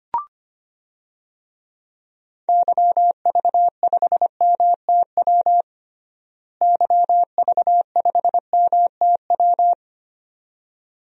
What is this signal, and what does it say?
Morse code, heard as a keyed tone, about 25 wpm, 700 Hz. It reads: YV5MTW YV5MTW
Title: DXCC-100 Call Signs - 6 Characters - Repeated 2x - 25wpm